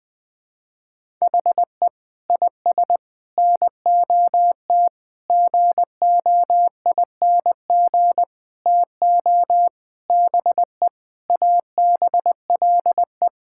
Morse code, heard as a keyed tone, about 21 wpm, 700 Hz. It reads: HE IS NOT GOING TO BE ABLE
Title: Sentences with 6-Word Phrases - 20wpm